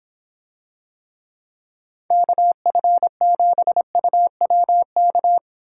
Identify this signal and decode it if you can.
Morse code, heard as a keyed tone, about 25 wpm, 700 Hz. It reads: KF7UWK